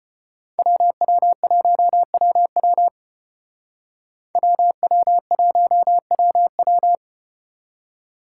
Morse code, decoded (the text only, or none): WW1WW WW1WW